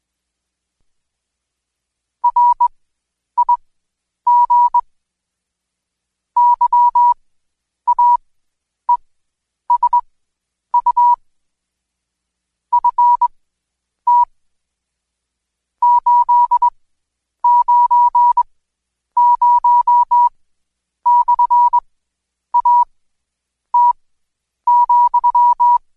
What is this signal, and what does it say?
Morse code, heard as a keyed tone, about 20 wpm, spaced as for 9 wpm, 970 Hz. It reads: RIG YAESU FT 890/AT,
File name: Code Quick / AudioDownld2 / Seg116-Day21-DayDrill 21